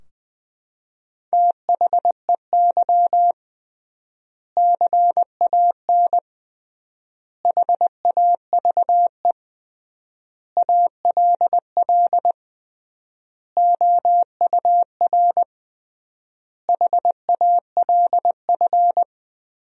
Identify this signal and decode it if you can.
Morse code, heard as a keyed tone, about 20 wpm, 700 Hz. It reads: THEY CAN HAVE ALL OUR HALF
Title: 3x Word-Spacing - Sentences from Top 400 Words - 20wpm